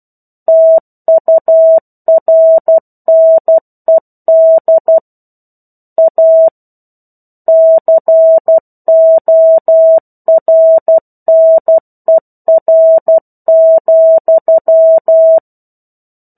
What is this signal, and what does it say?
Morse code, heard as a keyed tone, about 12 wpm, 650 Hz. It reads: TURNED A CORNER,
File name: 01-Chapter-01 12 WPM